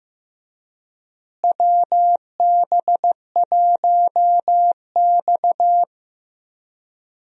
Morse code, heard as a keyed tone, about 15 wpm, 700 Hz. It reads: WB1X
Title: USA Call Signs - 4 Characters - Repeated 2x - 15wpm